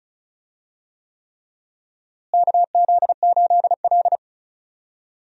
Morse code, heard as a keyed tone, about 34 wpm, 700 Hz. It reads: KZ8L
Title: USA Call Signs - 4 Characters - Repeated 3x - 35wpm